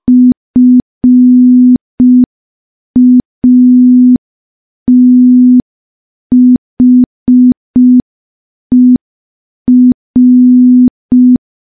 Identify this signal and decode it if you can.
Morse code, heard as a keyed tone, about 5 wpm, 250 Hz. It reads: FATHER
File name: morse / morse-training-05WPM